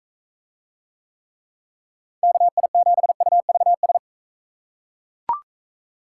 Morse code, heard as a keyed tone, about 40 wpm, 700 Hz. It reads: KI7UVS